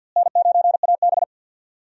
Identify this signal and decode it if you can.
Morse code, heard as a keyed tone, about 52 wpm, 700 Hz. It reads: N9AB
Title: USA Call Signs - 4 Characters - Repeated 2x - 50wpm